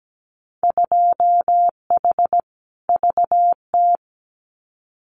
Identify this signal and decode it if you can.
Morse code, heard as a keyed tone, about 17 wpm, 700 Hz. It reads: 2H VT
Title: Winter Field Day Exchanges 17wpm